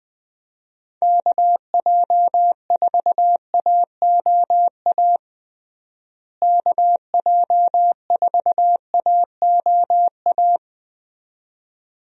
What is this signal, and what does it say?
Morse code, heard as a keyed tone, about 20 wpm, 700 Hz. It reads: KJ4AOA KJ4AOA
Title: USA Call Signs - 6 Characters - Repeated 2x - 20wpm